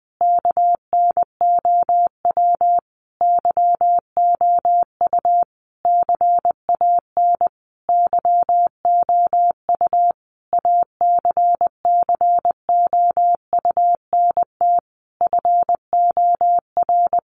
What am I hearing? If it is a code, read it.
Morse code, heard as a keyed tone, about 20 wpm, 700 Hz. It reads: KNOW YOU CAN YOU ACCOUNT FOR
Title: Sentences from Top 900 Words 20wpm